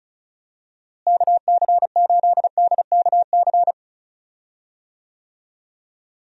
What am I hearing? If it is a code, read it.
Morse code, heard as a keyed tone, about 36 wpm, 700 Hz. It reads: KC8DKC
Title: USA Call Signs - 6 Characters - Repeated 3x - 35wpm